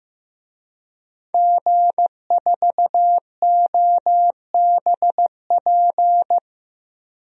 Morse code, heard as a keyed tone, about 15 wpm, 700 Hz. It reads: G4OBP
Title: DXCC-100 Call Signs - 5 Characters - Repeated 2x - 15wpm